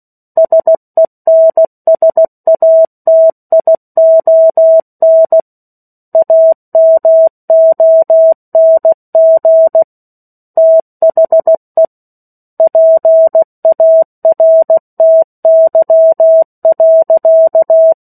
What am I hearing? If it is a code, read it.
Morse code, heard as a keyed tone, about 16 wpm, 650 Hz. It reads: SENSATION AMONG THE PARTY.
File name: 03-Chapter-03 16 WPM